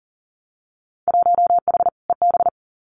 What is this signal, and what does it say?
Morse code, heard as a keyed tone, about 38 wpm, 700 Hz. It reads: 1H EB
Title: Winter Field Day Exchanges Encore 40wpm